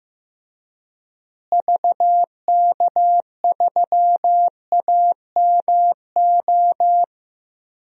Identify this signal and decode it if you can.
Morse code, heard as a keyed tone, about 15 wpm, 700 Hz. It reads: VK3AMO
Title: DXCC-100 Call Signs - 6 Characters - Repeated 3x - 15wpm